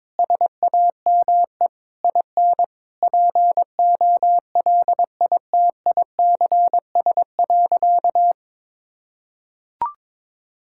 Morse code, heard as a keyed tone, about 22 wpm, 700 Hz. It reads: SAME IN POLITICS.
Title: Sentences from Top 1000 Words - Review Part 6 - 22wpm